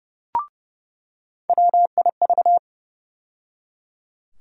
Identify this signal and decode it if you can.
Morse code, heard as a keyed tone, about 30 wpm, 700 Hz. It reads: WIV
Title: USA Call Sign Postfixes - 3 Letters 30wpm